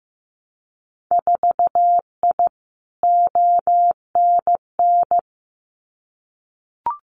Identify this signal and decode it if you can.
Morse code, heard as a keyed tone, about 15 wpm, 700 Hz. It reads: 4I ONN
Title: Winter Field Day Exchanges Encore 15wpm